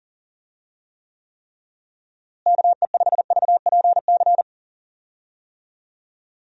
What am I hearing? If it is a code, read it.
Morse code, heard as a keyed tone, about 38 wpm, 700 Hz. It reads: KE5VPC
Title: USA Call Signs - 6 Characters - Repeated 3x - 40wpm